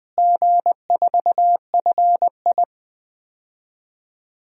Morse code, heard as a keyed tone, about 20 wpm, 700 Hz. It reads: G4FI
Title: DXCC-100 Call Signs - 4 Characters - Repeated 3x - 20wpm